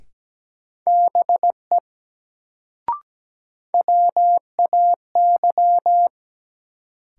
Morse code, heard as a keyed tone, about 17 wpm, 700 Hz. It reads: BE WAY
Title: Sentences from Top 100 Words - New Word Drill - 17wpm